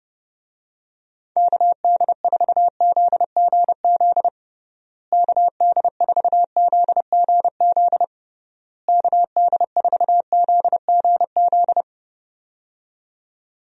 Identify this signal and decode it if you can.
Morse code, heard as a keyed tone, about 30 wpm, 700 Hz. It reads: KD4ZGZ KD4ZGZ KD4ZGZ
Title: USA Call Signs - 6 Characters - Repeated 3x - 30wpm